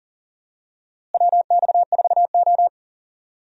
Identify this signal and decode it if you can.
Morse code, heard as a keyed tone, about 40 wpm, 700 Hz. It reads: WX4O
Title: USA Call Signs - 4 Characters - Repeated 2x - 40wpm